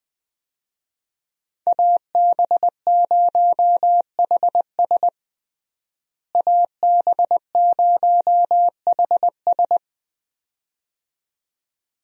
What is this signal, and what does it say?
Morse code, heard as a keyed tone, about 20 wpm, 700 Hz. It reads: AB0HS AB0HS
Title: USA Call Signs - 5 Characters - Repeated 2x - 20wpm